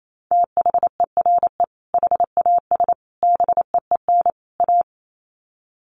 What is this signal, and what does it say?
Morse code, heard as a keyed tone, about 28 wpm, 700 Hz. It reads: THERE HAS BEEN A
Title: Sets of 4 Words 28wpm